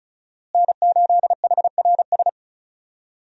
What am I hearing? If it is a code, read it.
Morse code, heard as a keyed tone, about 36 wpm, 700 Hz. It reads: N8HRS
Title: USA Call Signs - 5 Characters - Repeated 3x - 35wpm